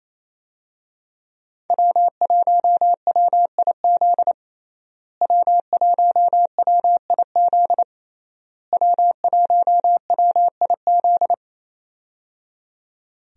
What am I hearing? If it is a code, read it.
Morse code, heard as a keyed tone, about 28 wpm, 700 Hz. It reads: W1WIZ W1WIZ W1WIZ
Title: USA Call Signs - 5 Characters - Repeated 3x - 28wpm